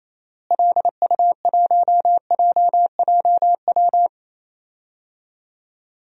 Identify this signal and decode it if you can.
Morse code, heard as a keyed tone, about 28 wpm, 700 Hz. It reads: LU1JJW
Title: DXCC-100 Call Signs - 6 Characters - Repeated 3x - 28wpm